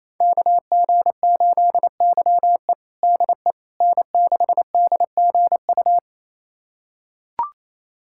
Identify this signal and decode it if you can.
Morse code, heard as a keyed tone, about 28 wpm, 700 Hz. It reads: KG8YE DE N6DGU
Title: USA Callsign DE Callsign - 5 Characters - 28wpm